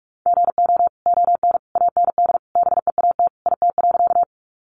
Morse code, heard as a keyed tone, about 44 wpm, 700 Hz. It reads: GO ON AND BEAT IT.
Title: Sentences from Top 1300 Words 45wpm